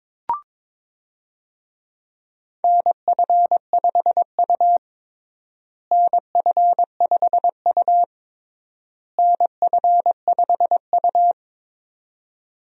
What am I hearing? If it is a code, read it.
Morse code, heard as a keyed tone, about 22 wpm, 700 Hz. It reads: NF5U NF5U NF5U
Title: USA Call Signs - 4 Characters - Repeated 3x - 22wpm